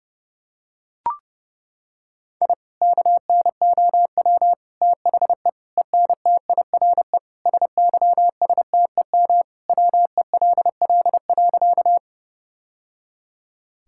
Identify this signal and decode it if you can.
Morse code, heard as a keyed tone, about 30 wpm, 700 Hz. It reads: I KNOW THE ENTIRE SYSTEM WELL.
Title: Sentences from Top 1000 Words - Review Part 5 - 30wpm